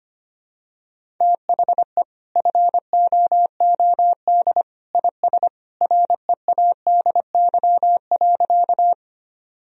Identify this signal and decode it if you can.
Morse code, heard as a keyed tone, about 25 wpm, 700 Hz. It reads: THE FOOD IS READY.
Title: Sentences from Top 1000 Words - Review Part 1 - 25wpm